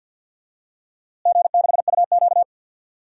Morse code, heard as a keyed tone, about 48 wpm, 700 Hz. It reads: G6UQ
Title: DXCC-100 Call Signs - 4 Characters - Repeated 3x - 50wpm